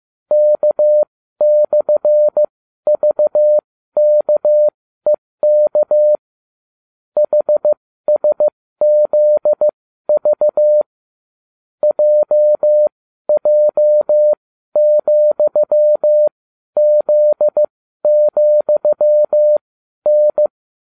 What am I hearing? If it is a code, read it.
Morse code, heard as a keyed tone, about 15 wpm, spaced as for 12 wpm, 600 Hz. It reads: K/VKEK HSZV JJ,Z,N
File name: cw / ubacw31 1